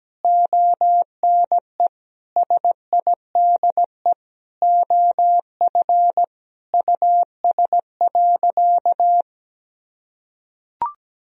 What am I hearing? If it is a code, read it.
Morse code, heard as a keyed tone, about 17 wpm, 700 Hz. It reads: ONE SIDE OF US.
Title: Sentences from Top 1000 Words - Review Part 3 - 17wpm